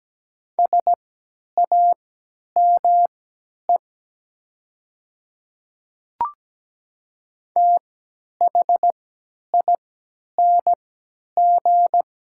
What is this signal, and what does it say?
Morse code, heard as a keyed tone, about 17 wpm, spaced as for 10 wpm, 700 Hz. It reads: SAME THING
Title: 3x Character Spacing - Top 200 Words - 17wpm